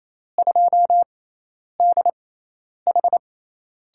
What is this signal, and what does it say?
Morse code, heard as a keyed tone, about 27 wpm, spaced as for 10 wpm, 700 Hz. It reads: 2DH
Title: USA Call Signs - 5 Characters - 6x Character Spacing - 28wpm